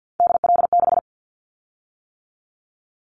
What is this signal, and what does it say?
Morse code, heard as a keyed tone, about 50 wpm, 700 Hz. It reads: DL6
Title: Callsign Prefixes for Top 25 DXCC - 50wpm